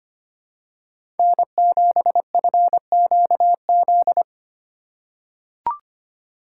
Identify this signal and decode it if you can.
Morse code, heard as a keyed tone, about 25 wpm, 700 Hz. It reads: N7FQZ
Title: USA Call Signs - 4x Character Spacing - 25wpm